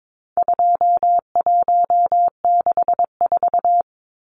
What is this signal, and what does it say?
Morse code, heard as a keyed tone, about 22 wpm, 700 Hz. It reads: 2164
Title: Numbers - 4-Digit 22wpm